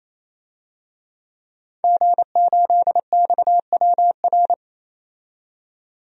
Morse code, heard as a keyed tone, about 28 wpm, 700 Hz. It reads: G8XWR
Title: DXCC-100 Call Signs - 5 Characters - 28wpm